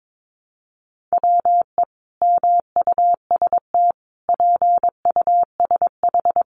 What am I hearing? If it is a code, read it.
Morse code, heard as a keyed tone, about 22 wpm, 700 Hz. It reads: WE MUST PUSH
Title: Sentences from Top 1300 Words 22wpm